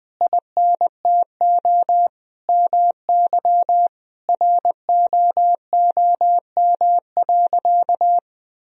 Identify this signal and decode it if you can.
Morse code, heard as a keyed tone, about 20 wpm, 700 Hz. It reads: INTO MY ROOM.
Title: Sentences from Top 1000 Words - Review Part 1 - 20wpm